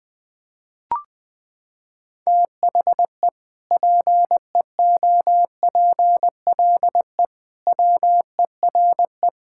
Morse code, heard as a keyed tone, about 20 wpm, 700 Hz. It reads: THE PEOPLE WERE
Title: Sentences from Top 1600 Words 20wpm